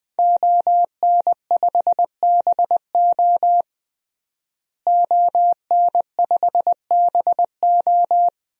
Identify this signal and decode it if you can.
Morse code, heard as a keyed tone, about 20 wpm, 700 Hz. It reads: ON5BO ON5BO
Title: DXCC-100 Call Signs - 5 Characters - Repeated 2x - 20wpm